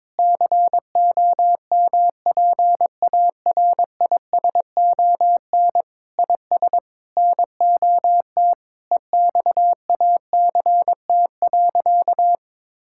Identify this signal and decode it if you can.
Morse code, heard as a keyed tone, about 22 wpm, 700 Hz. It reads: COMPARISON IS NOT EXACT.
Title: Sentences from Top 1600 Words 22wpm